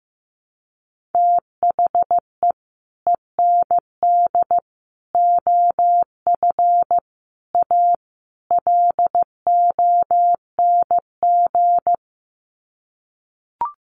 Morse code, heard as a keyed tone, about 15 wpm, 700 Hz. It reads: THE END OF A LONG
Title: Sets of 5 Words Encore2 15wpm